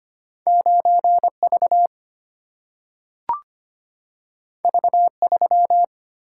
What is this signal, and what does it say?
Morse code, heard as a keyed tone, about 25 wpm, 700 Hz. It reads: 9V V3
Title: Single-Letter Single-Number 25wpm